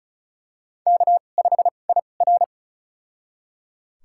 Morse code, heard as a keyed tone, about 36 wpm, spaced as for 25 wpm, 700 Hz. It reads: K5IR
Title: USA Call Signs - 2x Character Spacing - 35wpm